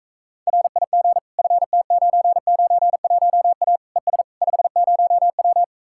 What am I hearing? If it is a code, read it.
Morse code, heard as a keyed tone, about 45 wpm, 700 Hz. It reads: RIG FT991A ES 50W